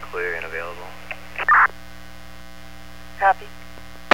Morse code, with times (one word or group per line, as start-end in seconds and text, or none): none